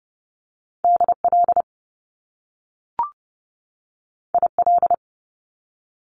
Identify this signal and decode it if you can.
Morse code, heard as a keyed tone, about 30 wpm, 700 Hz. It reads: DL IL
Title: Top Two Letter Combinations in Words 30wpm